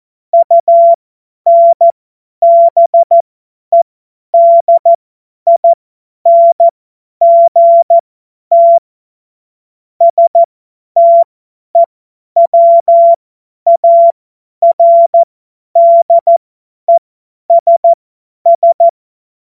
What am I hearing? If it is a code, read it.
Morse code, heard as a keyed tone, about 14 wpm, spaced as for 10 wpm, 680 Hz. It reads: UNBEDINGT STEWARDESS
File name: Morsekurs / Witze / Witz 06 Februar 2024 50 F70